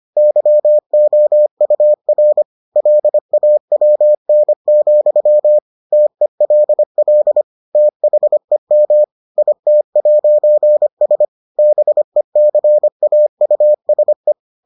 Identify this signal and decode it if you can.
Morse code, heard as a keyed tone, about 25 wpm, 600 Hz. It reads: YOUR LAWN, TELL THEM IT'S BECAUSE